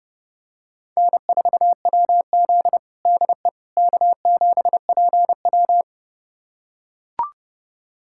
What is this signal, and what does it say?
Morse code, heard as a keyed tone, about 30 wpm, 700 Hz. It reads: N4WZ DE K7PW
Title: USA Callsign DE Callsign - 4 Characters - 30wpm